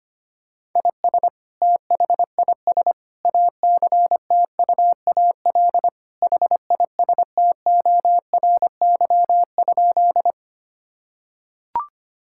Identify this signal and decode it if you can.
Morse code, heard as a keyed tone, about 25 wpm, 700 Hz. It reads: IS THIS ACTUAL HISTORY?
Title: Sentences from Top 1000 Words - Review Part 5 - 25wpm